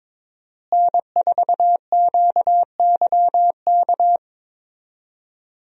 Morse code, heard as a keyed tone, about 21 wpm, 700 Hz. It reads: N4QYK
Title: USA Call Signs - 5 Characters - Repeated 2x - 22wpm